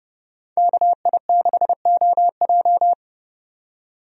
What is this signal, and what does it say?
Morse code, heard as a keyed tone, about 31 wpm, 700 Hz. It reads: KI6OJ
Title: USA Call Signs - 5 Characters - 30wpm